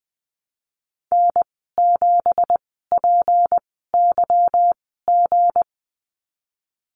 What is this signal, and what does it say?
Morse code, heard as a keyed tone, about 20 wpm, spaced as for 15 wpm, 700 Hz. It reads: N7PYG
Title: 179 Lesson 29g - Slash - Call Signs 20wpm